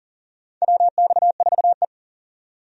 Morse code, heard as a keyed tone, about 40 wpm, 700 Hz. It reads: WX4E
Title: USA Call Signs - 4 Characters - 40wpm